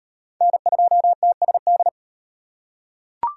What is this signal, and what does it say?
Morse code, heard as a keyed tone, about 38 wpm, 700 Hz. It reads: N2TSD